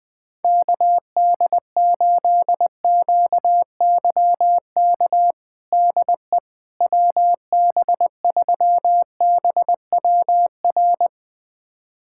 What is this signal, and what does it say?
Morse code, heard as a keyed tone, about 20 wpm, 700 Hz. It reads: KD8QYK DE WB3BWR
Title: USA Callsign DE Callsign 20wpm